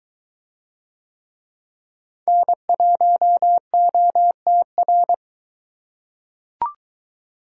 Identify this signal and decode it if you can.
Morse code, heard as a keyed tone, about 20 wpm, 700 Hz. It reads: N1OTR